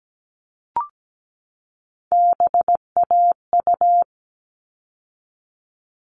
Morse code, heard as a keyed tone, about 17 wpm, 700 Hz. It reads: BAU
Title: All Cut Numbers - 3-Digit 17wpm